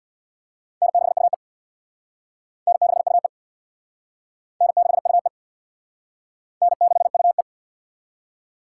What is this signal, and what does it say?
Morse code, heard as a keyed tone, about 64 wpm, 700 Hz. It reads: N6UE N6UE N6UE N6UE